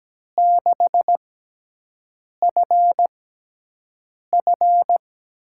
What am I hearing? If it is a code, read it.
Morse code, heard as a keyed tone, about 17 wpm, spaced as for 6 wpm, 700 Hz. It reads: 6FF
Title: USA Call Signs - 6 Characters - 6x Character Spacing - 17wpm